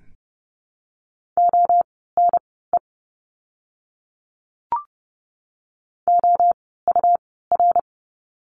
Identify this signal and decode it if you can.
Morse code, heard as a keyed tone, about 30 wpm, spaced as for 17 wpm, 700 Hz. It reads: ONE OUR
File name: course / 30:10wpm / 237 Lesson 38f - Letter q - Words 30wpm